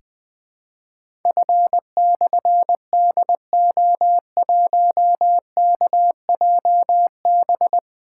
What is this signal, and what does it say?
Morse code, heard as a keyed tone, about 20 wpm, 700 Hz. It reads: F/DO1KJB
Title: Difficult Call Signs 20wpm